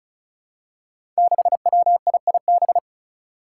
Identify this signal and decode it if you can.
Morse code, heard as a keyed tone, about 35 wpm, 700 Hz. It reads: 6WIIB
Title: Five Challenging Letter-Number Combinations 35wpm